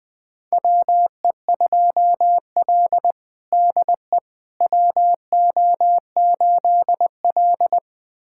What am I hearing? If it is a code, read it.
Morse code, heard as a keyed tone, about 20 wpm, 700 Hz. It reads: WE2L DE WO8L
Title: USA Callsign DE Callsign - 4 Characters - 20wpm